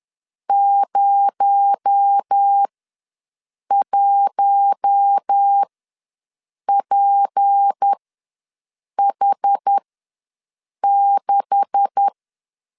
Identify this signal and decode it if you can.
Morse code, heard as a keyed tone, about 10 wpm, spaced as for 6 wpm, 800 Hz. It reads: 01PH6
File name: LEZIONI / 10 SIGNS / 10B INT ws 10wpm 4134